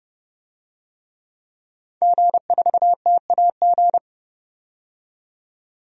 Morse code, heard as a keyed tone, about 30 wpm, 700 Hz. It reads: G4TAG
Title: DXCC-100 Call Signs - 5 Characters - 30wpm